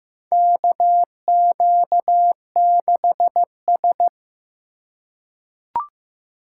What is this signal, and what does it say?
Morse code, heard as a keyed tone, about 16 wpm, 700 Hz. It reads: KQ6S
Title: USA Call Signs - 4 Characters - 2x Character Spacing - 15wpm